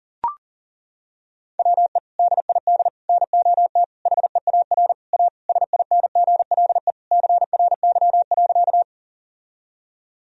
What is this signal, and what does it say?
Morse code, heard as a keyed tone, about 40 wpm, 700 Hz. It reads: WE DID NOT HEAR A SINGLE CRY.
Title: Sentences from Top 1000 Words - Review Part 4 - 40wpm